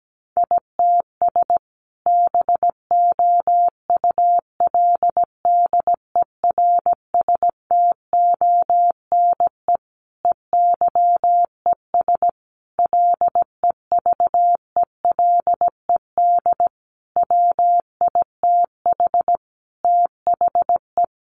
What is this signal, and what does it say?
Morse code, heard as a keyed tone, about 17 wpm, 700 Hz. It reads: ITS BOULDERSTONE EYES LEVELED WITH THE
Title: A Sound Of Thunder 17wpm